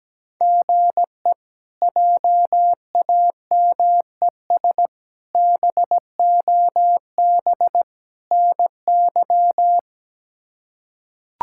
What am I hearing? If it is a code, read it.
Morse code, heard as a keyed tone, about 17 wpm, 700 Hz. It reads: GE JAMES BOB NY
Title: K1USN SST QSO Practice - 17wpm